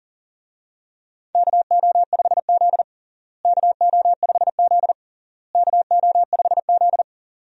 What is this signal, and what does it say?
Morse code, heard as a keyed tone, about 40 wpm, 700 Hz. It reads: KO5Z KO5Z KO5Z
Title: USA Call Signs - 4 Characters - Repeated 3x - 40wpm